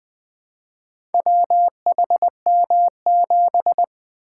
Morse code, heard as a keyed tone, about 20 wpm, 700 Hz. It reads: WHM7
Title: Four Letter-Number Combinations - 20wpm